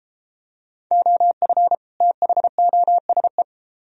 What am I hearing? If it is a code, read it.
Morse code, heard as a keyed tone, about 32 wpm, 700 Hz. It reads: OF THOSE